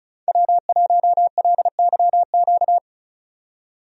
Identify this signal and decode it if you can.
Morse code, heard as a keyed tone, about 36 wpm, 700 Hz. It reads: W1LYQ
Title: USA Call Signs - 5 Characters - 35wpm